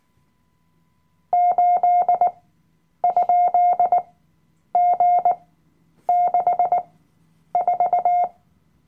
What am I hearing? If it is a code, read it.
Morse code, heard as a keyed tone, about 18 wpm, spaced as for 9 wpm, 710 Hz. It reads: 8?G64
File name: Workshop Dateien / CW-Online-Kurs / Lektion 007